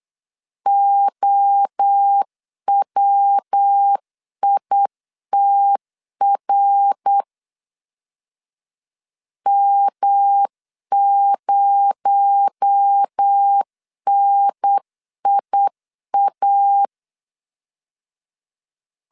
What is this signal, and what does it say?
Morse code, heard as a keyed tone, about 8 wpm, 800 Hz. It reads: OWITR M0NIA
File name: LEZIONI / 7 GF9Z / 7 G 8wpm 3118